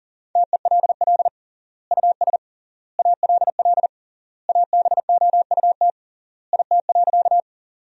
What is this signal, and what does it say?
Morse code, heard as a keyed tone, about 40 wpm, 700 Hz. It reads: TELL US ALL ABOUT IT.
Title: 3x Word-Spacing - Sentences from Top 200 Words - 40wpm